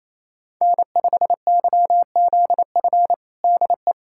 Morse code, heard as a keyed tone, about 28 wpm, 700 Hz. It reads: N5YZF DE
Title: USA Callsign DE Callsign - 5 Characters - 28wpm